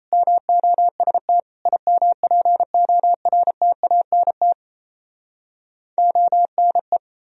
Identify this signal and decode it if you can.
Morse code, heard as a keyed tone, about 37 wpm, 700 Hz. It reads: MOST IMPORTANT ONE